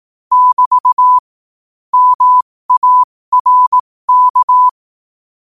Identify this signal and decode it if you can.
Morse code, heard as a keyed tone, about 18 wpm, spaced as for 15 wpm, 1000 Hz. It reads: <BT> MARK
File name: qotdfiles / 15wpm